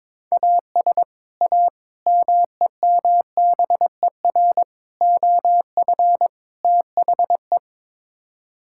Morse code, heard as a keyed tone, about 22 wpm, 700 Hz. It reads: AS A MEMBER OF THE
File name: practice / 22wpm / Build A Phrase - 5-Word Phrases - 22wpm